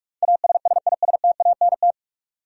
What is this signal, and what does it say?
Morse code, heard as a keyed tone, about 43 wpm, 700 Hz. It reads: ASSISTANT
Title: Practice QSO - Exchange 3 - More Info - 45wpm